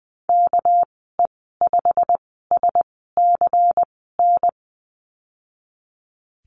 Morse code, heard as a keyed tone, about 20 wpm, spaced as for 15 wpm, 700 Hz. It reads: KE5SCN
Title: 200 Lesson 32g - Letter k - Call Signs 20wpm